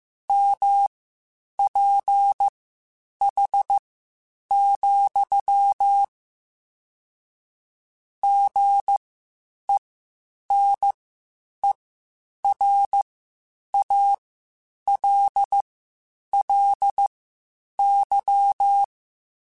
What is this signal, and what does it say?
Morse code, heard as a keyed tone, about 15 wpm, spaced as for 8 wpm, 800 Hz. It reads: MPH, GENERALLY
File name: Spaced Characters / Spaced Characters-mp3s / Weather5 1 sp @13